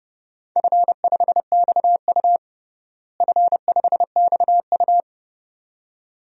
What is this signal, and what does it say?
Morse code, heard as a keyed tone, about 30 wpm, 700 Hz. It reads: F5XU F5XU
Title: DXCC-100 Call Signs - 4 Characters - Repeated 2x - 30wpm